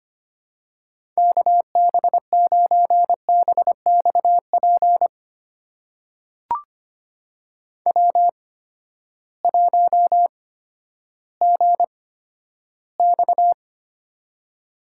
Morse code, heard as a keyed tone, about 25 wpm, spaced as for 12 wpm, 700 Hz. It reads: KB9BXP W1GX
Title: USA Call Signs - 8x Character Spacing - 25wpm